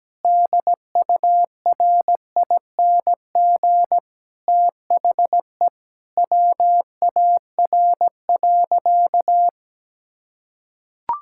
Morse code, heard as a keyed tone, about 17 wpm, 700 Hz. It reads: DURING THE WAR.
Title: Sentences from Top 1000 Words - Review Part 6 - 17wpm